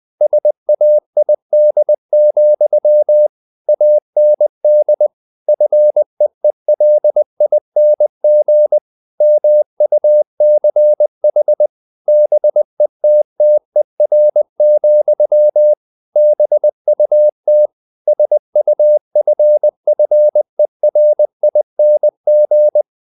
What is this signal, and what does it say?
Morse code, heard as a keyed tone, about 20 wpm, 600 Hz. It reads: SAID, AND FEELING MUCH BETTER, BUT SUFFERING
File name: the old man and the sea / the old man and the sea part 1 20wpm